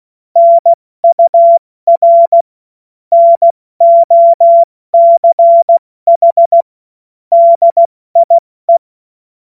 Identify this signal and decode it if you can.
Morse code, heard as a keyed tone, about 16 wpm, 680 Hz. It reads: NUR NOCH DIE
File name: Witz-28-Oktober-2025-70-F80